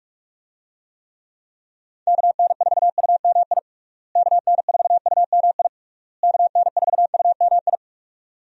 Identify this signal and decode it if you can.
Morse code, heard as a keyed tone, about 41 wpm, 700 Hz. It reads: KN4UMI KN4UMI KN4UMI
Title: USA Call Signs - 6 Characters - Repeated 3x - 45wpm